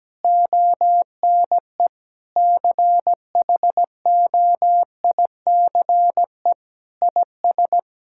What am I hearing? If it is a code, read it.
Morse code, heard as a keyed tone, about 17 wpm, 700 Hz. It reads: ONE CHOICE IS
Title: Sentences from Top 1000 Words - Review Part 6 - 17wpm